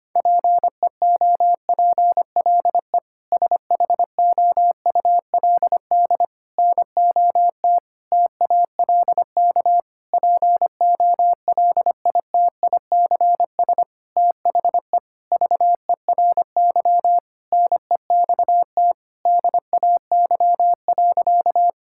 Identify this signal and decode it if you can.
Morse code, heard as a keyed tone, about 25 wpm, 700 Hz. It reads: PEOPLE SHOULD NOT TALK POLITICS THE VERY NEXT DAY.
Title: Sentences from Top 1000 Words - Review Part 6 - 25wpm